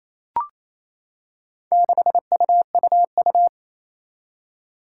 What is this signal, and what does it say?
Morse code, heard as a keyed tone, about 29 wpm, 700 Hz. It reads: 6UUU
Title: Four Challenging Letter-Number Combinations 28wpm